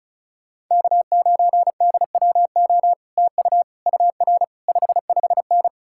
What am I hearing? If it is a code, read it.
Morse code, heard as a keyed tone, about 48 wpm, spaced as for 42 wpm, 700 Hz. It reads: K9DWO TU UR 55N
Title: Speed-Racing Common QSO Phrases Encore - 35wpm